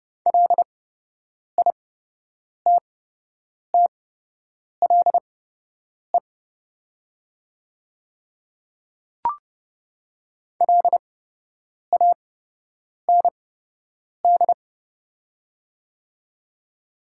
Morse code, heard as a keyed tone, about 30 wpm, spaced as for 8 wpm, 690 Hz. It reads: LITTLE LAND
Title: 8x Character Spacing - Top 200 Words - 30wpm